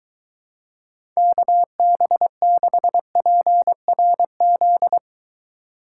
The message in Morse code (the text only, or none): KB6PRZ